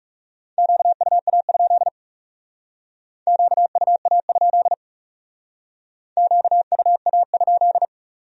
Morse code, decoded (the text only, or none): QUA? QUA? QUA?